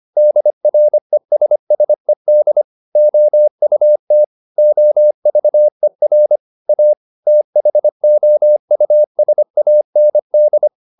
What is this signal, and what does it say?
Morse code, heard as a keyed tone, about 25 wpm, 600 Hz. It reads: DRESSED OUT OVER A THOUSAND
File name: the old man and the sea part 1 25wpm